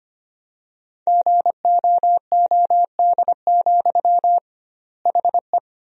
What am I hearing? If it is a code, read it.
Morse code, heard as a keyed tone, about 25 wpm, 700 Hz. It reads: GOOD, HE
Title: 2x Word-Spacing - Sentences from Top 200 Words - 25wpm